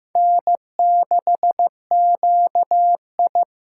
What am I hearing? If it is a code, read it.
Morse code, heard as a keyed tone, about 15 wpm, 700 Hz. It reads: N6QI
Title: USA Call Signs - 4 Characters - 8x Character Spacing - 15wpm